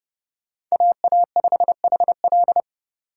Morse code, heard as a keyed tone, about 36 wpm, 700 Hz. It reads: AA5HL
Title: Speed Racing - USA 2x2 Call Signs 30wpm